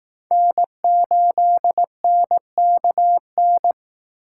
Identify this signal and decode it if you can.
Morse code, heard as a keyed tone, about 19 wpm, 700 Hz. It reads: N8NKN
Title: Speed Racing - USA 1x3 Call Signs 15wpm